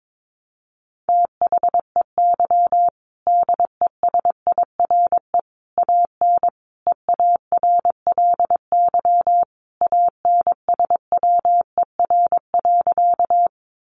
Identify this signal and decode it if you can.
Morse code, heard as a keyed tone, about 22 wpm, 700 Hz. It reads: THEY DESIRE AN EARLY ANSWER.
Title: Sentences from Top 1100 Words 22wpm